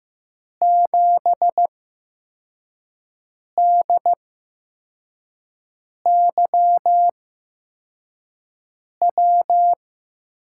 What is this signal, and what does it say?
Morse code, heard as a keyed tone, about 15 wpm, spaced as for 4 wpm, 700 Hz. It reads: 7DYW
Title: USA Call Signs - 6 Characters - 8x Character Spacing - 15wpm